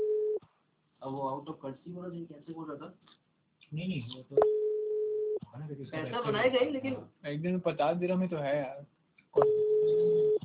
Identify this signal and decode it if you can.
Ringback tone (Europe, 425 Hz), with speech.